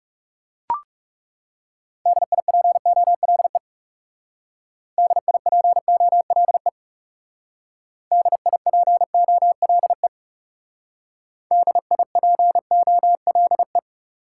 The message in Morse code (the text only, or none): DIPOLE DIPOLE DIPOLE DIPOLE